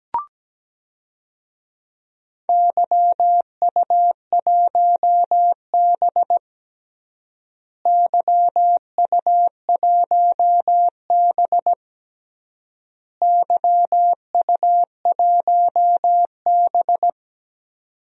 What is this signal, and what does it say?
Morse code, heard as a keyed tone, about 17 wpm, 700 Hz. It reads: YU1B YU1B YU1B
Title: DXCC-100 Call Signs - 4 Characters - Repeated 3x - 17wpm